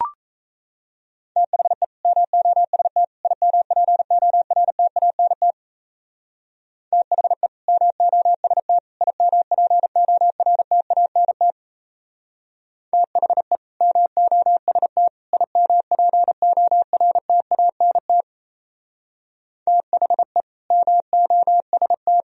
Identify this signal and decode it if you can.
Morse code, heard as a keyed tone, about 33 wpm, 700 Hz. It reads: THE MOST IMPORTANT THE MOST IMPORTANT THE MOST IMPORTANT THE MOST
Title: Speed-Racing - Sets of 3 Words 28wpm